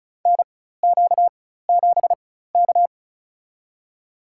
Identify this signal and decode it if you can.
Morse code, heard as a keyed tone, about 36 wpm, spaced as for 16 wpm, 700 Hz. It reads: NQ7K
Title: USA Call Signs - 4 Characters - 4x Character Spacing - 35wpm